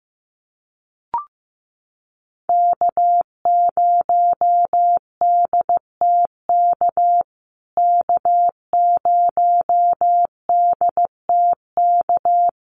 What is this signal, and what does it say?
Morse code, heard as a keyed tone, about 15 wpm, 700 Hz. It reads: K0DTK K0DTK
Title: USA Call Signs Repeated 3x 15wpm